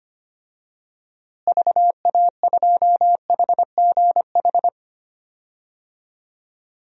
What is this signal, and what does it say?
Morse code, heard as a keyed tone, about 25 wpm, 700 Hz. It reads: VA2HGH
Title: DXCC-100 Call Signs - 6 Characters - Repeated 2x - 25wpm